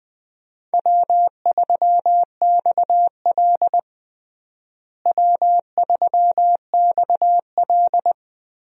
Morse code, heard as a keyed tone, about 20 wpm, 700 Hz. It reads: W3XL W3XL
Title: USA Call Signs - 4 Characters - Repeated 3x - 20wpm